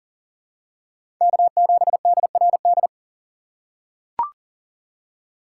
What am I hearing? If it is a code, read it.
Morse code, heard as a keyed tone, about 39 wpm, 700 Hz. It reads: K7DRD
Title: USA Call Signs - 5 Characters - 6x Character Spacing - 40wpm